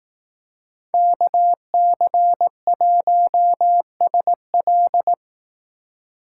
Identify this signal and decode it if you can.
Morse code, heard as a keyed tone, about 18 wpm, 700 Hz. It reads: KC1SL
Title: Speed Racing - USA Call Signs 15wpm